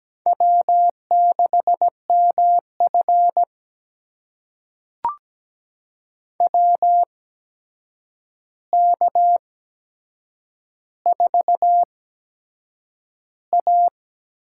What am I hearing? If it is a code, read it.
Morse code, heard as a keyed tone, about 17 wpm, spaced as for 7 wpm, 700 Hz. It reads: W6MF WK4A
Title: USA Call Signs - 4 Characters - 8x Character Spacing - 17wpm